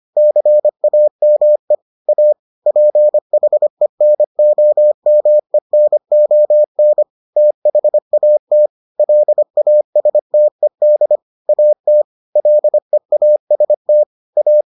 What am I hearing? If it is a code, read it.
Morse code, heard as a keyed tone, about 25 wpm, 600 Hz. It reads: CAME A PHENOMENON THAT LASTED AT LEAST A